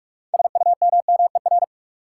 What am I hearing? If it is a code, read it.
Morse code, heard as a keyed tone, about 41 wpm, 700 Hz. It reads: SUMMER
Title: Sentences from Top 1100 Words - New Word Drill Alt - 45wpm